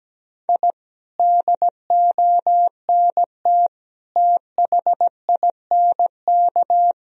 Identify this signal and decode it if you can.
Morse code, heard as a keyed tone, about 17 wpm, 700 Hz. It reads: I DONT THINK
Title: Sentences from Top 1000 Words - Review Part 4 - 17wpm